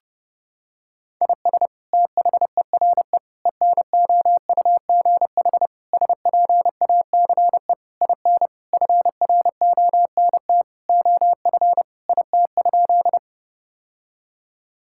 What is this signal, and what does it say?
Morse code, heard as a keyed tone, about 30 wpm, 700 Hz. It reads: IS THERE ENOUGH SPACE IN FRONT OF IT?
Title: Sentences from Top 1000 Words - Review Part 5 - 30wpm